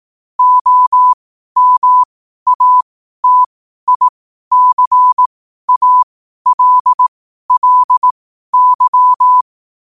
Morse code, heard as a keyed tone, about 18 wpm, spaced as for 12 wpm, 1000 Hz. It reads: OMATICALLY
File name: qotdfiles / 12wpm